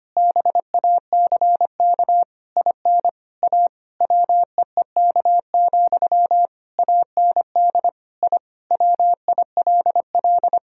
Morse code, heard as a keyed tone, about 25 wpm, 700 Hz. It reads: BACK IN A WEEK, AND I WILL
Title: Sentences from Top 1000 Words - Review Part 3 - 25wpm